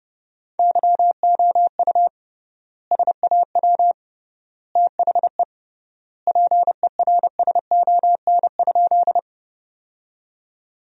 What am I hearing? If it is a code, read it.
Morse code, heard as a keyed tone, about 30 wpm, 700 Hz. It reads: YOU SAW THE PERSON?
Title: 3x Word-Spacing - Sentences from Top 400 Words - 30wpm